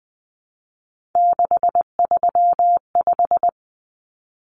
Morse code, heard as a keyed tone, about 20 wpm, 700 Hz. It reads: 635
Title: Common Cut Numbers - 3-Digit 20wpm